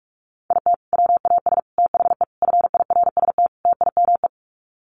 Morse code, heard as a keyed tone, about 44 wpm, 700 Hz. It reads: IT WAS THE FIRST TIME
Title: Sets of 5 Words 45wpm